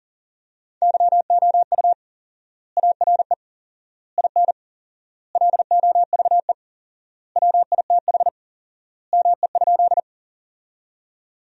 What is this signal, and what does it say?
Morse code, heard as a keyed tone, about 40 wpm, 700 Hz. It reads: YOU ARE IN LOVE WITH ME?
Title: 4x Word-Spacing - Sentences from Top 400 Words - 40wpm